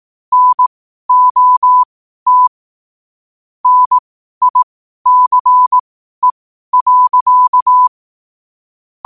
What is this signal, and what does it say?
Morse code, heard as a keyed tone, about 18 wpm, spaced as for 12 wpm, 1000 Hz. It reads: NOT NICE.
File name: qotdfiles / 12wpm